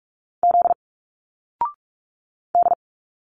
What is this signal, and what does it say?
Morse code, heard as a keyed tone, about 44 wpm, 700 Hz. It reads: Z D
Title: Single Letters 45wpm